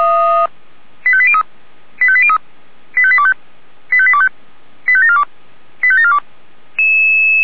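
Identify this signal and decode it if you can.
Five-tone selective call (ZVEI), heard as quick runs of five tones, tones of about 70 ms. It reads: 85803 85803 85825 85825 85742 85742
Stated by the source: ZVEI five-tone selective call 85730, received with an SDR.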